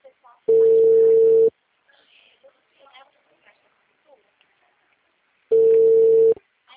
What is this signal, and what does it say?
Ringback tone (China, 450 Hz), with speech.